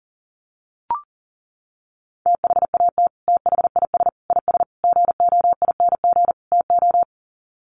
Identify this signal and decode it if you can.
Morse code, heard as a keyed tone, about 40 wpm, 700 Hz. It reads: THAT THIS IS GOING TO
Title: Sets of 5 Words Encore2 40wpm